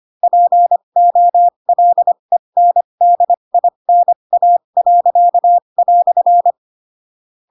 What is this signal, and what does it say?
Morse code, heard as a keyed tone, about 25 wpm, spaced as for 22 wpm, 700 Hz. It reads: POLENDINA."
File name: Pinocchio / Pinocchio ch2 18 25 ik1hge